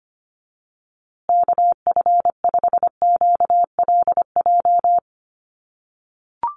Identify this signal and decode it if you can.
Morse code, heard as a keyed tone, about 25 wpm, 700 Hz. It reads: KF5QLJ
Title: USA Call Signs Repeated 3x 25wpm